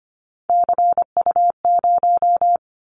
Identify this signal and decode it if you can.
Morse code, heard as a keyed tone, about 25 wpm, 700 Hz. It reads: CU0
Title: Callsign Prefixes for Top 100 DXCC - 25wpm